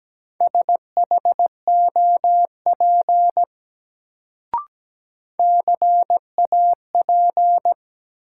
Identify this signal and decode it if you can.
Morse code, heard as a keyed tone, about 17 wpm, 700 Hz. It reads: SHOP CAP
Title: Sentences from Top 1600 Words - New Word Drill Alt - 17wpm